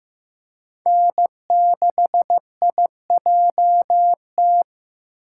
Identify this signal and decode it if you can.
Morse code, heard as a keyed tone, about 15 wpm, 700 Hz. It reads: N6IJT
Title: USA Call Signs - 5 Characters - 4x Character Spacing - 15wpm